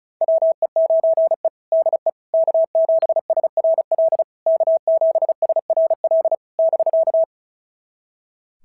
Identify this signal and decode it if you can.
Morse code, heard as a keyed tone, about 33 wpm, 660 Hz. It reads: WE9E DE K7SRL K7SRL <BK>
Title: SOTA QSO Practice 35wpm